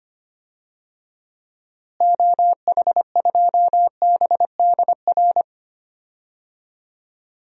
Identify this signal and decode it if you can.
Morse code, heard as a keyed tone, about 25 wpm, 700 Hz. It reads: OH2BDR
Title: DXCC-100 Call Signs - 6 Characters - Repeated 2x - 25wpm